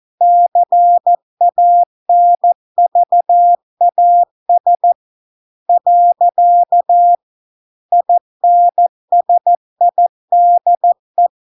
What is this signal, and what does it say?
Morse code, heard as a keyed tone, about 14 wpm, 700 Hz. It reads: CANVAS . INSIDE